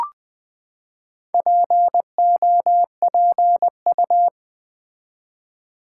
Morse code, heard as a keyed tone, about 20 wpm, 700 Hz. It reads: POPU
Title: Build A Word - 4 Letters At A Time - 10-Letter Words - 20wpm